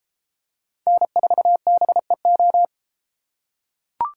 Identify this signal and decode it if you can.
Morse code, heard as a keyed tone, about 30 wpm, 700 Hz. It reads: N4BEO